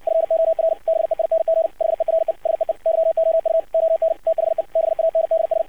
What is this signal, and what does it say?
Morse code, heard as a keyed tone, about 54 wpm, 650 Hz, with radio noise. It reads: FOR BEING HERE SEE YOU ON THE BANDS
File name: archived / CRSnet